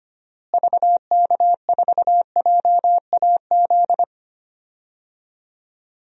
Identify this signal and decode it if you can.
Morse code, heard as a keyed tone, about 25 wpm, 700 Hz. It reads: VK4JAZ
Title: DXCC-100 Call Signs - 6 Characters - Repeated 3x - 25wpm